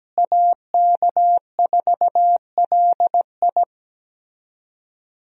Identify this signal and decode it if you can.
Morse code, heard as a keyed tone, about 17 wpm, 700 Hz. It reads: AK4LI
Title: USA Call Signs - 5 Characters - Repeated 3x - 17wpm